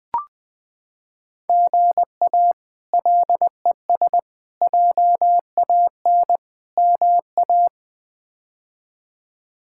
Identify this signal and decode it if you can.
Morse code, heard as a keyed tone, about 20 wpm, 700 Hz. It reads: GA LES JAN MA
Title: K1USN SST QSO Practice - 20wpm